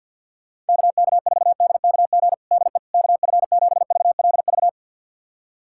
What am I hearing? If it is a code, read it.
Morse code, heard as a keyed tone, about 47 wpm, 690 Hz. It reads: KK4DKG DE KF7VLV
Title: USA Callsign DE Callsign 50wpm